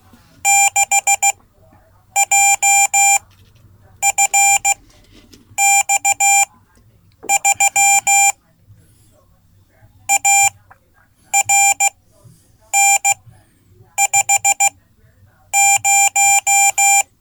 Morse code, heard as a keyed tone, about 16 wpm, spaced as for 8 wpm, 790 Hz. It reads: 6JFX3 ARN50
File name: Code / 5min-c